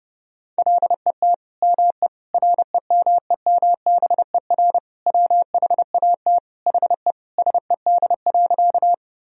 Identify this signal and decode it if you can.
Morse code, heard as a keyed tone, about 32 wpm, 700 Hz. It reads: LET ME REMEMBER WHAT HE SED.
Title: Sentences from Top 400 Words and Abbreviations 30wpm